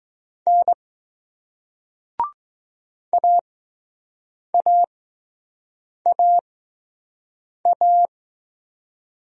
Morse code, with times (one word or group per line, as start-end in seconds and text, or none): 0.5-0.7 N
3.1-8.1 AAAA